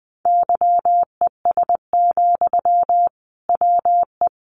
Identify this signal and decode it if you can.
Morse code, heard as a keyed tone, about 19 wpm, 700 Hz. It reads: YES, WE
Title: Sentences from Top 1500 Words 20wpm